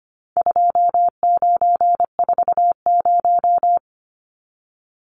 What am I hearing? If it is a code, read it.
Morse code, heard as a keyed tone, about 25 wpm, 700 Hz. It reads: 2940
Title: Numbers - 4-Digit 25wpm